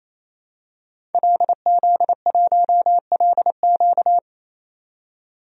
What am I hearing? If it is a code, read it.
Morse code, heard as a keyed tone, about 28 wpm, 700 Hz. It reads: LZ1LQ
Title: DXCC-100 Call Signs - 5 Characters - 28wpm